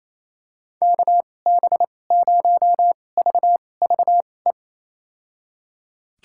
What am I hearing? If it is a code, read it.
Morse code, heard as a keyed tone, about 28 wpm, spaced as for 20 wpm, 700 Hz. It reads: KB0VVE
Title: USA Call Signs - 6 Characters - 2x Character Spacing - 28wpm